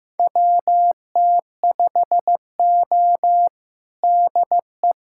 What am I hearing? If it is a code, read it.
Morse code, heard as a keyed tone, about 15 wpm, 700 Hz. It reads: WT5O DE
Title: USA Callsign DE Callsign - 4 Characters - 15wpm